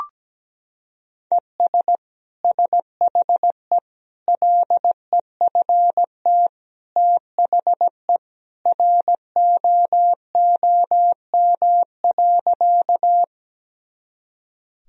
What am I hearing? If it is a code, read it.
Morse code, heard as a keyed tone, about 17 wpm, 700 Hz. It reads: ES SHE LEFT THE ROOM.
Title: Sentences from Top 300 Words and Abbreviations 17wpm